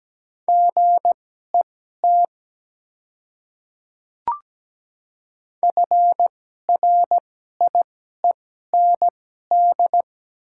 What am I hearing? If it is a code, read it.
Morse code, heard as a keyed tone, about 17 wpm, spaced as for 12 wpm, 700 Hz. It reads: GET FRIEND
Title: 2x Character Spacing - Top 300 Words - 17wpm